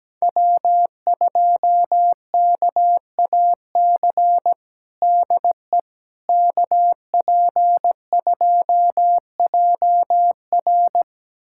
Morse code, heard as a keyed tone, about 17 wpm, 700 Hz. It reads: W2KAC DE KP2JR
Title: USA Callsign DE Callsign - 5 Characters - 17wpm